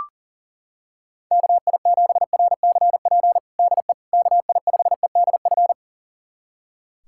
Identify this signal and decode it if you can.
Morse code, heard as a keyed tone, about 37 wpm, 700 Hz. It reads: KI7RCP DE KI5EDF
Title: USA Callsign DE Callsign 40wpm